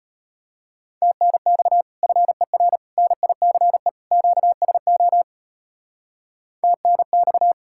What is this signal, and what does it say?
Morse code, heard as a keyed tone, about 32 wpm, 700 Hz. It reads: TNX FER NICE QSO TNX